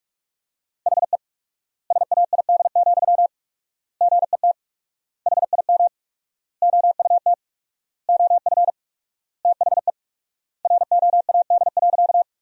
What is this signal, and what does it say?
Morse code, heard as a keyed tone, about 46 wpm, 700 Hz. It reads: HE SAID, GET HIM OUT OF THE ROAD.
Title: 4x Word-Spacing - Sentences from Top 400 Words - 45wpm